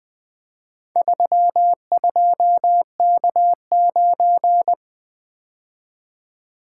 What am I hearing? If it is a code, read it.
Morse code, heard as a keyed tone, about 20 wpm, 700 Hz. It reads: 32K9
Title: Four Letter-Number Combinations - 20wpm